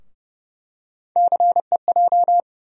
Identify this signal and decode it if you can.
Morse code, heard as a keyed tone, about 30 wpm, 700 Hz. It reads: CEJ
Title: USA Call Sign Postfixes - 3 Letters 30wpm